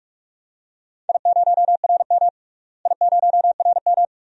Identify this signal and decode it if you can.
Morse code, heard as a keyed tone, about 41 wpm, 700 Hz. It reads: I0RM I0RM
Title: DXCC-100 Call Signs - 4 Characters - Repeated 2x - 45wpm